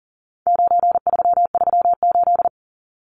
Morse code, heard as a keyed tone, about 40 wpm, 700 Hz. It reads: 9338
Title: Numbers - 4-Digit 40wpm